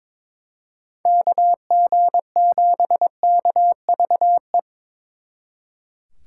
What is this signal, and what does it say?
Morse code, heard as a keyed tone, about 25 wpm, 700 Hz. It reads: KG7KVE
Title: Speed Racing - USA 2x3 Call Signs 22wpm